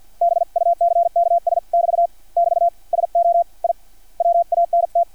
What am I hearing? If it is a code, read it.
Morse code, heard as a keyed tone, about 50 wpm, spaced as for 44 wpm, 690 Hz. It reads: BUCKS<BT> <BT> SO I WANT
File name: archived / CRSnet